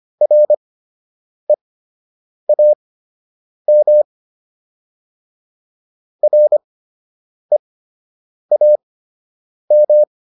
Morse code, heard as a keyed tone, about 25 wpm, spaced as for 8 wpm, 600 Hz. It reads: REAM REAM